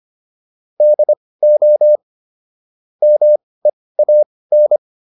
Morse code, heard as a keyed tone, about 25 wpm, spaced as for 18 wpm, 600 Hz. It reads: DO MEAN